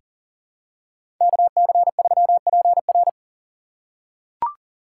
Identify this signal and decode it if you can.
Morse code, heard as a keyed tone, about 40 wpm, 700 Hz. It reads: KC3PR
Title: USA Call Signs - 5 Characters - 2x Character Spacing - 40wpm